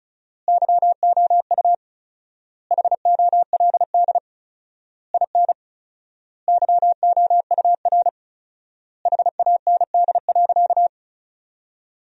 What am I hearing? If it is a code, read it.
Morse code, heard as a keyed tone, about 36 wpm, 700 Hz. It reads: YOU HOLD IN YOUR HAND.
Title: 4x Word-Spacing - Sentences from Top 400 Words - 35wpm